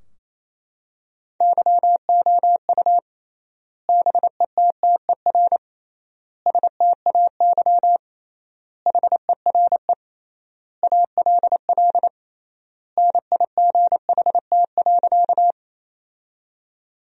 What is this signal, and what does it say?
Morse code, heard as a keyed tone, about 28 wpm, 700 Hz. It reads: YOU BETTER STAY HERE ALL NIGHT.
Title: 3x Word-Spacing - Sentences from Top 500 Words - 28wpm